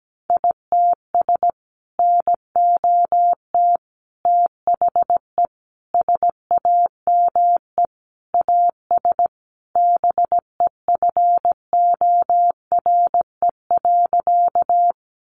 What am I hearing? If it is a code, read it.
Morse code, heard as a keyed tone, about 17 wpm, 700 Hz. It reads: ITS NOT THE SAME AS BEFORE.
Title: Sentences from Top 600 Words 17wpm